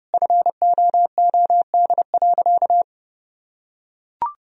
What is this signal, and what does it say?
Morse code, heard as a keyed tone, about 30 wpm, 700 Hz. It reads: FOOD.
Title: Sentences from Top 1000 Words - Review Part 3 - 30wpm